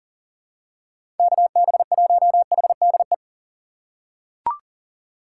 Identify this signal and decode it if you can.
Morse code, heard as a keyed tone, about 40 wpm, 700 Hz. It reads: KB1HDE
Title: USA Call Signs - 6 Characters - 6x Character Spacing - 40wpm